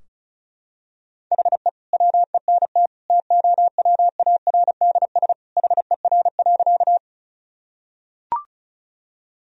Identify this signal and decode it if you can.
Morse code, heard as a keyed tone, about 36 wpm, 700 Hz. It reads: HE WENT TOWARDS HER.
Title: Sentences from Top 1000 Words - Review Part 2 - 35wpm